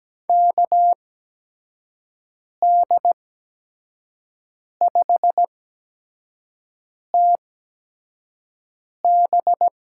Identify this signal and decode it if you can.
Morse code, heard as a keyed tone, about 17 wpm, spaced as for 5 wpm, 700 Hz. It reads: KD5TB
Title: USA Call Signs - 5 Characters - 8x Character Spacing - 17wpm